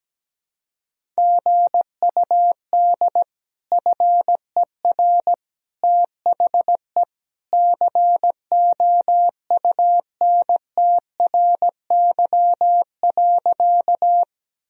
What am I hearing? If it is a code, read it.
Morse code, heard as a keyed tone, about 17 wpm, 700 Hz. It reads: GUD FER THE COUNTRY.
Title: Sentences from Top 300 Words and Abbreviations 17wpm